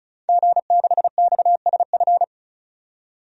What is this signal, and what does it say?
Morse code, heard as a keyed tone, about 36 wpm, 700 Hz. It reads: G6XSF
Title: DXCC-100 Call Signs - 5 Characters - 35wpm